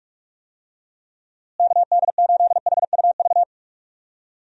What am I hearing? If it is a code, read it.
Morse code, heard as a keyed tone, about 41 wpm, 700 Hz. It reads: KD8HUV
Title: USA Call Signs - 6 Characters - Repeated 2x - 45wpm